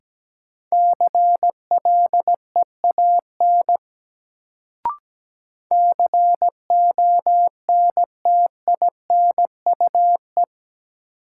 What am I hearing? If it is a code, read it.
Morse code, heard as a keyed tone, about 17 wpm, 700 Hz. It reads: CLEAN CONTINUE
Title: Sentences from Top 1100 Words - New Word Drill Alt - 17wpm